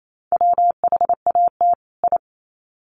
Morse code, heard as a keyed tone, about 28 wpm, 700 Hz. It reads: WHAT I
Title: Sets of 2 Words 28wpm